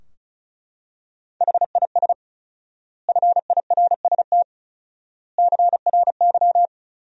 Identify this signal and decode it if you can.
Morse code, heard as a keyed tone, about 34 wpm, 700 Hz. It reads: HIS FIRST CRY
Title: 4x Word-Spacing - Sentences from Top 500 Words - 35wpm